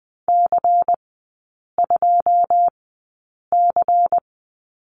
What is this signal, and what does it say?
Morse code, heard as a keyed tone, about 20 wpm, spaced as for 8 wpm, 700 Hz. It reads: C2C
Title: 093 Lesson 16e - Letter c - Three Characters 20wpm